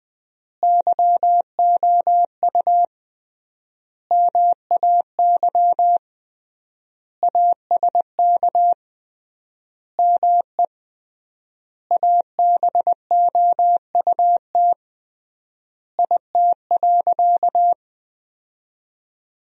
Morse code, heard as a keyed tone, about 20 wpm, 700 Hz. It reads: YOU MAY ASK ME ABOUT IT.
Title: 3x Word-Spacing - Sentences from Top 200 Words - 20wpm